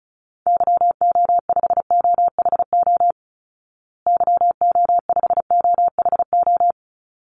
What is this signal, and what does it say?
Morse code, heard as a keyed tone, about 34 wpm, 700 Hz. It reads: YO5OHO YO5OHO
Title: Call Signs Repeated 3x - 35wpm